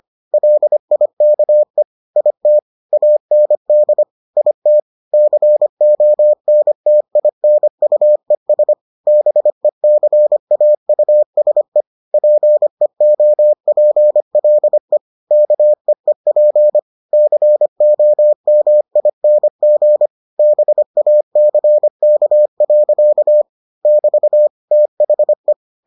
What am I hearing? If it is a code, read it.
Morse code, heard as a keyed tone, about 25 wpm, 600 Hz. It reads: LIKE IT AND IT CONTINUES BECAUSE PEOPLE KEEP COMING BACK. <BT> THE